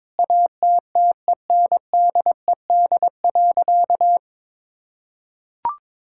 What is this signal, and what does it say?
Morse code, heard as a keyed tone, about 22 wpm, 700 Hz. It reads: ATTENDED.
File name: practice / 22wpm / Sentences with 4-Word Phrases - 22wpm